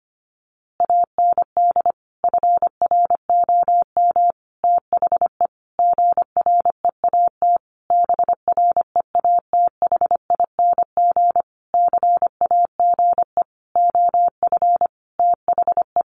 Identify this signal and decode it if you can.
Morse code, heard as a keyed tone, about 25 wpm, 700 Hz. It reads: AND FROM THE GREAT BREATHING CAGE OF THE
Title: A Sound Of Thunder 25wpm